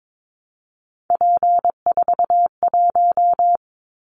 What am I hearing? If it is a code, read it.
Morse code, heard as a keyed tone, about 22 wpm, 700 Hz. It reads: P41
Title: Callsign Prefixes for Top 100 DXCC - 22wpm